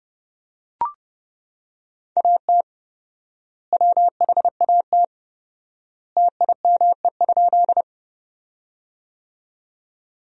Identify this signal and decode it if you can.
Morse code, heard as a keyed tone, about 30 wpm, 700 Hz. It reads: AT WHAT TIME?
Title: 4x Word-Spacing - Sentences from Top 100 Words - 30wpm